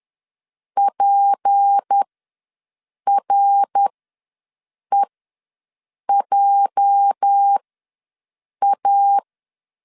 Morse code, heard as a keyed tone, about 10 wpm, spaced as for 6 wpm, 800 Hz. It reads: PREJA
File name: LEZIONI / 8 YJ45 / 8 YJ4 ws 10wpm 4020